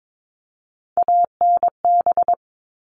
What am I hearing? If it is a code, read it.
Morse code, heard as a keyed tone, about 22 wpm, 700 Hz. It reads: ANB
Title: All Cut Numbers - 3-Digit 22wpm